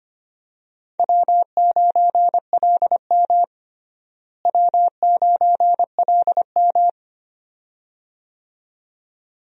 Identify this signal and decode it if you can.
Morse code, heard as a keyed tone, about 25 wpm, 700 Hz. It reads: W9LM W9LM
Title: USA Call Signs - 4 Characters - Repeated 2x - 25wpm